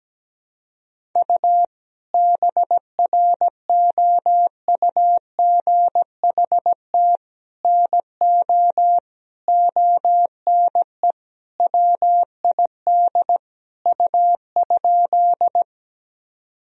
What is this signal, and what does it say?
Morse code, heard as a keyed tone, about 17 wpm, 700 Hz. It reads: U BROUGHT NO ONE WID U?
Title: Sentences from Top 500 Words and Abbreviations 17wpm